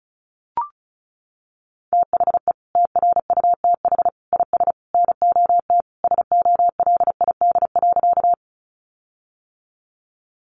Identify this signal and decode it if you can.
Morse code, heard as a keyed tone, about 34 wpm, 700 Hz. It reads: THE TRUTH IS NOT SOLID.
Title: Sentences from Top 1500 Words 35wpm